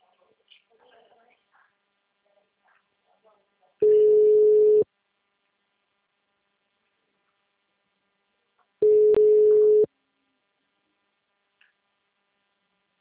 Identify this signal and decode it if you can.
Ringback tone (Europe, 425 Hz), with speech.
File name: out- -2033- - - .2018